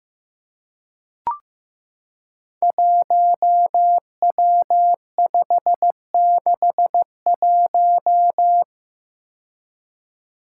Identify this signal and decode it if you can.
Morse code, heard as a keyed tone, about 15 wpm, 700 Hz. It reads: 1W561
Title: Five Challenging Letter-Number Combinations 15wpm